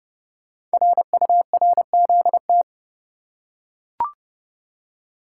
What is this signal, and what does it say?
Morse code, heard as a keyed tone, about 28 wpm, 700 Hz. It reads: RURZT
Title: Code Groups - 5 Letters - 30wpm